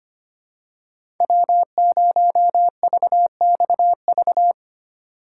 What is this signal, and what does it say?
Morse code, heard as a keyed tone, about 25 wpm, 700 Hz. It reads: W0VXV
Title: USA Call Signs - 5 Characters - 4x Character Spacing - 25wpm